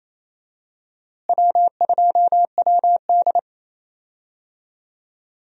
Morse code, heard as a keyed tone, about 28 wpm, 700 Hz. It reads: W2WD
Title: USA Call Signs - 4 Characters - Repeated 2x - 28wpm